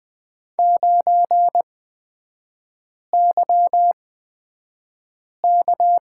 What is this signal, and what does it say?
Morse code, heard as a keyed tone, about 20 wpm, spaced as for 5 wpm, 700 Hz. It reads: 9YK
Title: 224 Lesson 36e - BT - Three Characters 20wpm